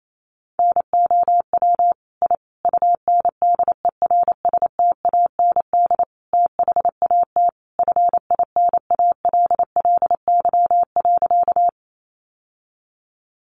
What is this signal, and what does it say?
Morse code, heard as a keyed tone, about 28 wpm, 700 Hz. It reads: NOW I UNDERSTAND THAT FINALLY.
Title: Sentences from Top 1000 Words 28wpm